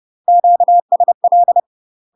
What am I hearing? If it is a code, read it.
Morse code, heard as a keyed tone, about 30 wpm, 700 Hz. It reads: QSL
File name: UK MASTER QSOs TRAINING/30WPM